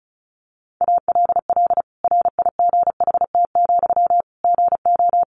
Morse code, heard as a keyed tone, about 34 wpm, 700 Hz. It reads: ALL RIGHT, GO
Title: Sentences from Top 1100 Words 35wpm